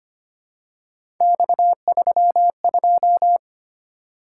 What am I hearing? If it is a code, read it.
Morse code, heard as a keyed tone, about 25 wpm, 700 Hz. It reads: X32
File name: Three Letter-Number Combinations - 25wpm